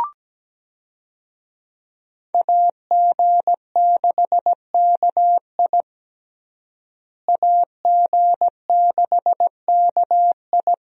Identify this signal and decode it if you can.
Morse code, heard as a keyed tone, about 17 wpm, 700 Hz. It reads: AG6KI AG6KI
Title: USA Call Signs - 5 Characters - Repeated 3x - 17wpm